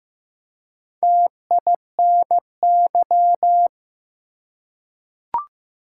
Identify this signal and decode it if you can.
Morse code, heard as a keyed tone, about 16 wpm, 700 Hz. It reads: TINY
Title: Four Letter Words - 15wpm